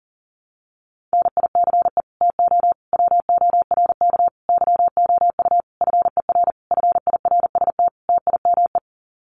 Morse code, heard as a keyed tone, about 40 wpm, 700 Hz. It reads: NICE TO WORK YOU FER FIRST TIME
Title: Common QSO Phrases 40wpm